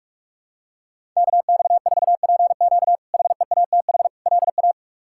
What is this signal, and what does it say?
Morse code, heard as a keyed tone, about 46 wpm, 700 Hz. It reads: KX4PQ HEATH LA
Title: K1USN SST QSO Practice - 45wpm